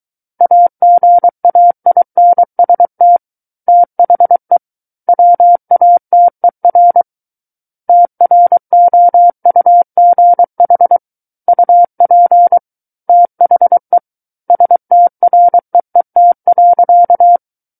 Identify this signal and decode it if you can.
Morse code, heard as a keyed tone, about 23 wpm, 700 Hz. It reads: AGAINST THE WATER TROUGH UP THE STREET.
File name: Book One / 14-Chapter 14 - 23 WPM